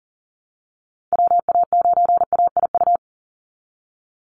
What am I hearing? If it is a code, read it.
Morse code, heard as a keyed tone, about 40 wpm, 700 Hz. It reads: WA9AIU
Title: USA Call Signs 40wpm